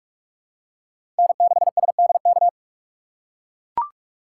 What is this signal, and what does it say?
Morse code, heard as a keyed tone, about 45 wpm, 700 Hz. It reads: N6SDK
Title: USA Call Signs - 5 Characters - 3x Character Spacing - 45wpm